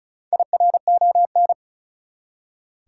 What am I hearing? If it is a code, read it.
Morse code, heard as a keyed tone, about 34 wpm, 700 Hz. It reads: IRON
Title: Four Letter Words - 35wpm